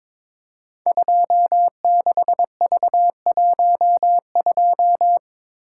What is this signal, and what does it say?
Morse code, heard as a keyed tone, about 22 wpm, 700 Hz. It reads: 26V12
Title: Code Groups - 5 Letter:Numbers - 22wpm